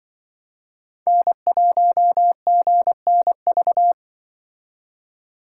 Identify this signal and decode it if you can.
Morse code, heard as a keyed tone, about 25 wpm, 700 Hz. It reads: N1GNV
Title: Speed Racing - USA 1x3 Call Signs 20wpm